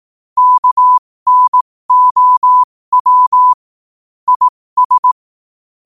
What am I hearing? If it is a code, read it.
Morse code, heard as a keyed tone, about 18 wpm, spaced as for 15 wpm, 1000 Hz. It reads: KNOW IS
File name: qotdfiles / 15wpm